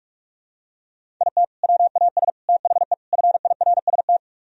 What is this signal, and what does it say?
Morse code, heard as a keyed tone, about 48 wpm, 700 Hz. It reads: IT WAS THE FIRST